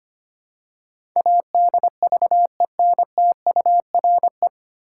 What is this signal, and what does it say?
Morse code, heard as a keyed tone, about 25 wpm, 700 Hz. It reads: ADVENTURE
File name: practice / 25wpm / 9 Letter Words - First 3 Letters Given - 25wpm